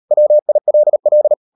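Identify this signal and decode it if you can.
Morse code, heard as a keyed tone, about 38 wpm, 600 Hz. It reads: WILL
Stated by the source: Morse code practice at 38 WPM, sending Ken Follett's novel Never in English, from CW med Gnister.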